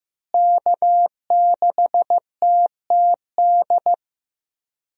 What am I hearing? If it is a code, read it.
Morse code, heard as a keyed tone, about 15 wpm, 700 Hz. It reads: K6TTD
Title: USA Call Signs - 5 Characters - Repeated 2x - 15wpm